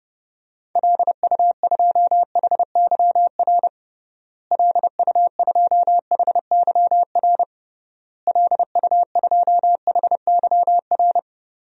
Morse code, heard as a keyed tone, about 30 wpm, 700 Hz. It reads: LU2HYR LU2HYR LU2HYR
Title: DXCC-100 Call Signs - 6 Characters - Repeated 3x - 30wpm